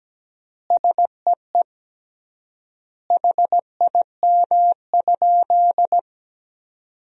Morse code, heard as a keyed tone, about 17 wpm, 700 Hz. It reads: SEE HIM?
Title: 3x Word-Spacing - Sentences from Top 100 Words - 17wpm